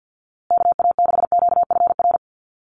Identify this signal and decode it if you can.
Morse code, heard as a keyed tone, about 50 wpm, 700 Hz. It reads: KA6QFR
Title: USA Call Signs Repeated 3x 50wpm